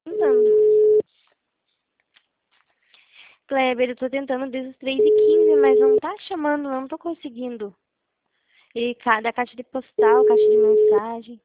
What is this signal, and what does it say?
Ringback tone (Europe, 425 Hz), with speech.